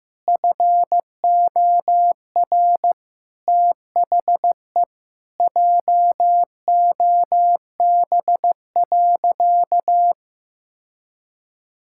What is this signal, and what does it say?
Morse code, heard as a keyed tone, about 15 wpm, 700 Hz. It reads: FOR THE JOB.
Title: Sentences from Top 1000 Words - Review Part 6 - 15wpm